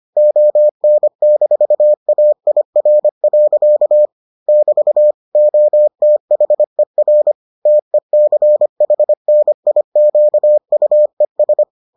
Morse code, heard as a keyed tone, about 25 wpm, 600 Hz. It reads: ON-AIR. <BT> OTHER TECHNIQUES